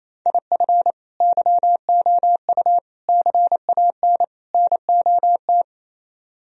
Morse code, heard as a keyed tone, about 34 wpm, 700 Hz. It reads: IF YOU CAN NOT
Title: Speed-Racing - Sets of 4 Words 28wpm